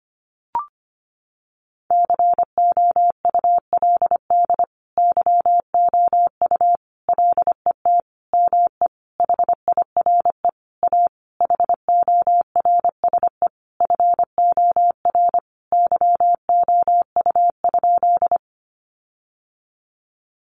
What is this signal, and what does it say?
Morse code, heard as a keyed tone, about 25 wpm, 700 Hz. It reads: COULD YOU LET ME HIRE A HORSE FOR YOU?
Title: Sentences from Top 1400 Words 25wpm